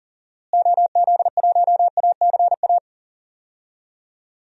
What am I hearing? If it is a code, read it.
Morse code, heard as a keyed tone, about 40 wpm, 700 Hz. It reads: OZ1ACA
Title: DXCC-100 Call Signs - 6 Characters - Repeated 3x - 40wpm